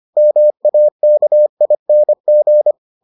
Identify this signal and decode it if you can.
Morse code, heard as a keyed tone, about 25 wpm, 600 Hz. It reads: MAKING